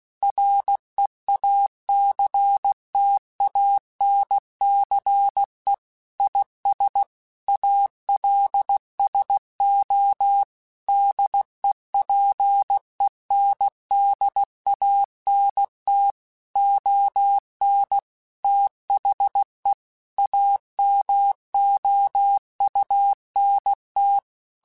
Morse code, heard as a keyed tone, about 16 wpm, 800 Hz. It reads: REACTANCE IS ALSO DEPENDANT ON THE AMOUNT